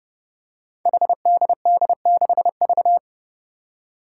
Morse code, heard as a keyed tone, about 30 wpm, 700 Hz. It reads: HDD6V
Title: Five Challenging Letter-Number Combinations 30wpm